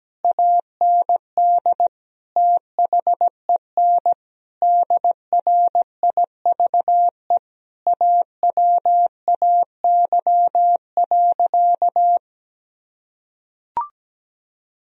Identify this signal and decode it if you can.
Morse code, heard as a keyed tone, about 17 wpm, 700 Hz. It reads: AND THEN DRIVE AWAY.
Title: Sentences from Top 1000 Words - Review Part 3 - 17wpm